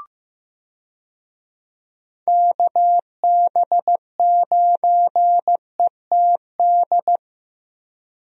Morse code, heard as a keyed tone, about 15 wpm, 700 Hz. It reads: KB9ETD
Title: USA Call Signs - 6 Characters - Repeated 3x - 15wpm